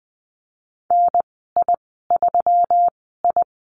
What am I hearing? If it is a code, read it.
Morse code, heard as a keyed tone, about 20 wpm, spaced as for 15 wpm, 700 Hz. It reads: NI3I
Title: 173 Lesson 28g - Number 9 - Call Signs 20wpm